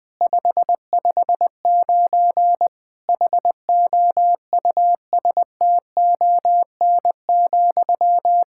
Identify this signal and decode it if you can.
Morse code, heard as a keyed tone, about 20 wpm, 700 Hz. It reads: 559 HOUSTON,
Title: K1USN SST QSO Practice 20wpm